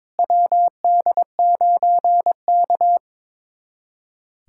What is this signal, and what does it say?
Morse code, heard as a keyed tone, about 22 wpm, 700 Hz. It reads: WD9K
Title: USA Callsign DE Callsign - 4 Characters - 22wpm